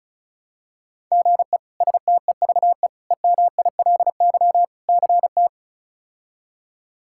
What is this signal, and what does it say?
Morse code, heard as a keyed tone, about 36 wpm, 700 Hz. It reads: GE STEVE EMILY CT
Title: K1USN SST QSO Practice - 35wpm